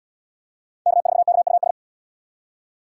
Morse code, heard as a keyed tone, about 72 wpm, 700 Hz. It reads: K4KUS